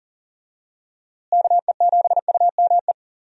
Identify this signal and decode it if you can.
Morse code, heard as a keyed tone, about 39 wpm, 700 Hz. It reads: KE7UME
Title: USA Call Signs - 6 Characters - Repeated 2x - 40wpm